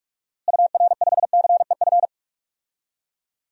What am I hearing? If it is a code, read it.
Morse code, heard as a keyed tone, about 46 wpm, 700 Hz. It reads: UR5CEF
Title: DXCC-100 Call Signs - 6 Characters - Repeated 3x - 45wpm